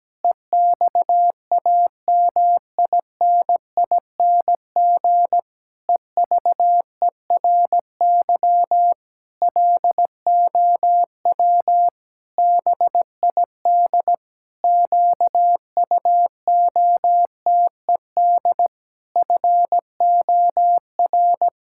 Morse code, heard as a keyed tone, about 17 wpm, 700 Hz. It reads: EXAMINING EVERY LOW BID QUOTED FOR
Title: Pangram Challenge v2 - 17wpm